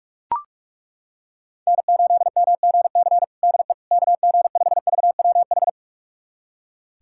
Morse code, heard as a keyed tone, about 41 wpm, 700 Hz. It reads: N8MGC DE KG5VWH
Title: USA Callsign DE Callsign 45wpm